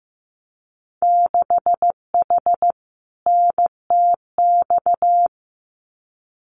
Morse code, heard as a keyed tone, about 15 wpm, 700 Hz. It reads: 6H NTX
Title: Winter Field Day Exchanges Encore 15wpm